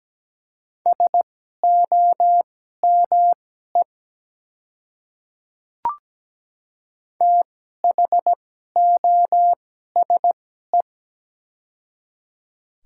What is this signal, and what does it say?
Morse code, heard as a keyed tone, about 17 wpm, spaced as for 12 wpm, 700 Hz. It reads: SOME THOSE
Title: 2x Character Spacing - Top 300 Words - 17wpm